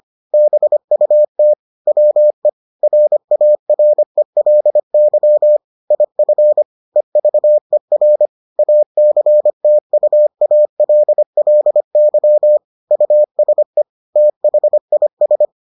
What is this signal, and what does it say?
Morse code, heard as a keyed tone, about 25 wpm, 600 Hz. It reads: BUT WE RARELY IF EVER ACTUALLY USE THIS